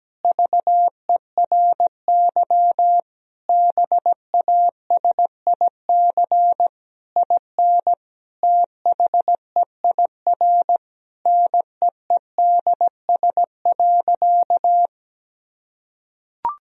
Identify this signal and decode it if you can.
Morse code, heard as a keyed tone, about 17 wpm, 700 Hz. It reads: VERY BASIC IN THEIR NEEDS.
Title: Sentences from Top 1000 Words - Review Part 3 - 17wpm